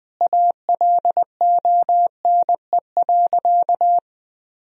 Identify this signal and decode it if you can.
Morse code, heard as a keyed tone, about 21 wpm, 700 Hz. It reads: ALONE.
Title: Sentences from Top 1000 Words - Review Part 2 - 20wpm